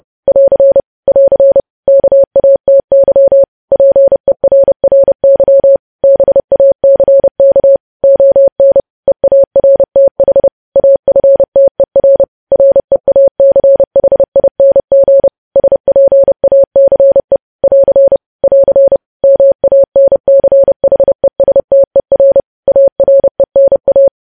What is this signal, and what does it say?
Morse code, heard as a keyed tone, about 30 wpm, 550 Hz. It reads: <AR> <AR> KATY PERRY BACK ON EARTH AFTER REACHING SPACE <AR> <AR> MANCHESTER ARENA
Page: news headlines in morse code at 30 wpm